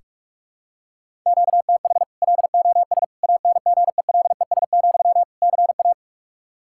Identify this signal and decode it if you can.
Morse code, heard as a keyed tone, about 46 wpm, 700 Hz. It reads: QTH LOS ANGELES, CA
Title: Practice QSO - Exchange 1 - Beginning - 45wpm